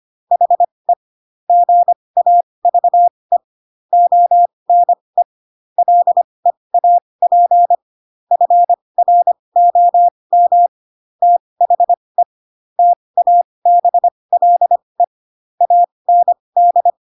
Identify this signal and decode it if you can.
Morse code, heard as a keyed tone, about 25 wpm, spaced as for 20 wpm, 700 Hz. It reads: HE GAVE ONE LEAP FROM THE TABLE AND
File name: Pinocchio / Pinocchio ch8 15 25 ik1hge